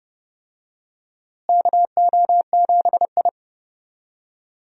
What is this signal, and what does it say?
Morse code, heard as a keyed tone, about 30 wpm, 700 Hz. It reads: KO7I
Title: USA Call Signs - 4 Characters - Repeated 2x - 30wpm